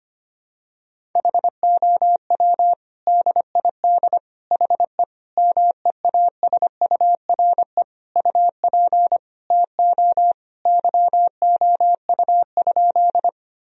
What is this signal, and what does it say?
Morse code, heard as a keyed tone, about 25 wpm, 700 Hz. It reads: HOW DID HE MEASURE UP TO YOU?
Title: Sentences from Top 1000 Words - Review Part 2 - 25wpm